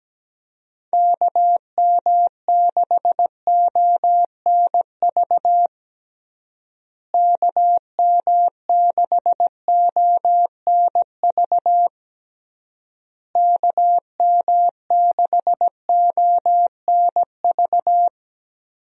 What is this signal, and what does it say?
Morse code, heard as a keyed tone, about 17 wpm, 700 Hz. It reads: KM6ONV KM6ONV KM6ONV
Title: USA Call Signs - 6 Characters - Repeated 3x - 17wpm